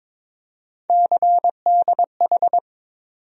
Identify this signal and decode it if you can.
Morse code, heard as a keyed tone, about 22 wpm, 700 Hz. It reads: CDH
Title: USA Call Sign Postfixes - 3 Letters 22wpm